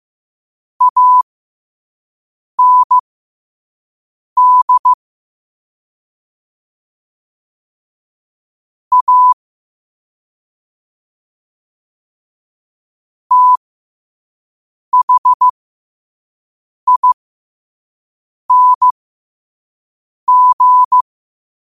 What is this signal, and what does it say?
Morse code, heard as a keyed tone, about 15 wpm, spaced as for 5 wpm, 1000 Hz. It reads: AND A THING
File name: qotdfiles / 5wpm